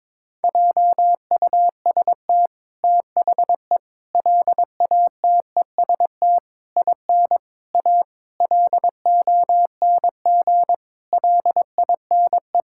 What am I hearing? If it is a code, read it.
Morse code, heard as a keyed tone, about 21 wpm, 700 Hz. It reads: JUST THE LATEST IN A LONG LINE
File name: practice / 22wpm / Sets of 8 Words - 22wpm